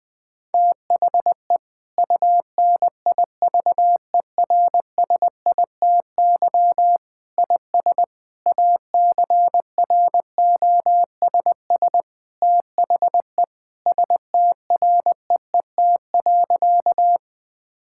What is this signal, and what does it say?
Morse code, heard as a keyed tone, about 20 wpm, 700 Hz. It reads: THE UNIVERSITY IS ACROSS THE STREET.
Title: Sentences from Top 1000 Words - Review Part 5 - 20wpm